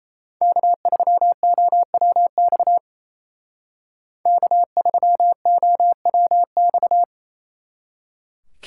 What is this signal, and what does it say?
Morse code, heard as a keyed tone, about 36 wpm, 700 Hz. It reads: K3OWX K3OWX